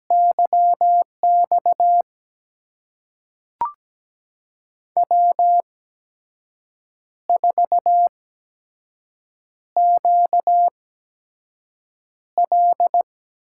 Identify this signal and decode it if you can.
Morse code, heard as a keyed tone, about 17 wpm, spaced as for 6 wpm, 700 Hz. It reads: YX W4QL
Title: USA Call Signs - 8x Character Spacing - 17wpm